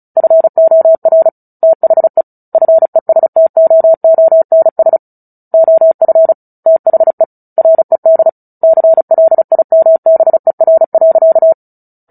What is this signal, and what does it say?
Morse code, heard as a keyed tone, about 34 wpm, 650 Hz. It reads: FOR THE FESTOONS OF THE RED CLIMBER.
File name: Book Two / 09-Chapter 9 - 35 WPM